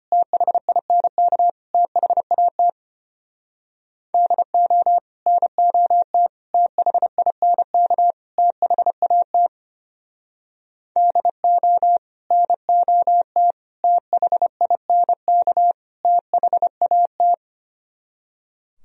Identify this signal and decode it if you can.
Morse code, heard as a keyed tone, about 32 wpm, 700 Hz. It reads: THINK THAT DO NOT THINK THAT DO NOT THINK THAT